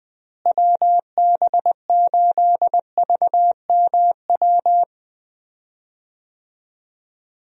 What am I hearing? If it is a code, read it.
Morse code, heard as a keyed tone, about 20 wpm, 700 Hz. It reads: WB8VMW
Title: USA Call Signs - 6 Characters - Repeated 2x - 20wpm